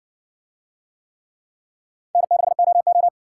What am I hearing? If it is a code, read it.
Morse code, heard as a keyed tone, about 54 wpm, 700 Hz. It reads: N6OO